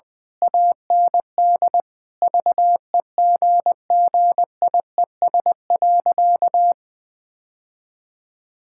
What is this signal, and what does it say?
Morse code, heard as a keyed tone, about 21 wpm, 700 Hz. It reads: AND VEGGIES.
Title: Easy Sentences with Hints - 20wpm